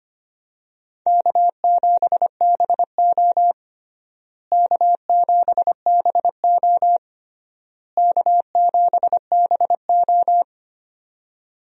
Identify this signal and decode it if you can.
Morse code, heard as a keyed tone, about 25 wpm, 700 Hz. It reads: K7BO K7BO K7BO
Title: USA Call Signs - 4 Characters - Repeated 3x - 25wpm